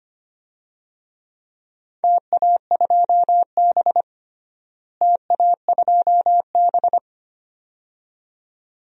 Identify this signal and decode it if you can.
Morse code, heard as a keyed tone, about 25 wpm, 700 Hz. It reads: TA2B TA2B
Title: DXCC-100 Call Signs - 4 Characters - Repeated 2x - 25wpm